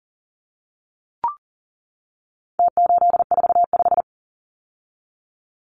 Morse code, heard as a keyed tone, about 40 wpm, 700 Hz. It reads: T845
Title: Numbers - Contest Style - 4-Digit 40wpm